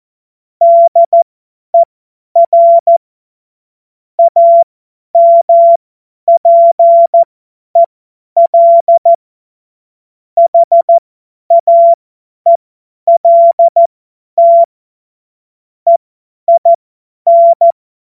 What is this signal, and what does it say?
Morse code, heard as a keyed tone, about 14 wpm, spaced as for 10 wpm, 680 Hz. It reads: DER AMPEL HAELT EIN
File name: Morsekurs / Witze / Witz 01. März 2023 50 F70